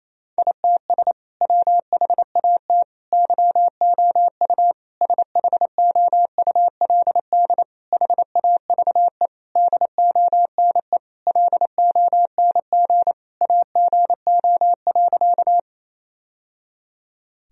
Morse code, heard as a keyed tone, about 27 wpm, 700 Hz. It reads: ITS WHAT YOU SHOULD HAVE DONE LONG AGO.
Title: Sentences from Top 1000 Words - Review Part 3 - 28wpm